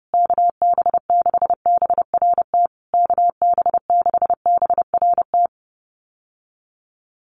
Morse code, heard as a keyed tone, about 30 wpm, 700 Hz. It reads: KB6BRT KB6BRT
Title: USA Call Signs Repeated 3x 30wpm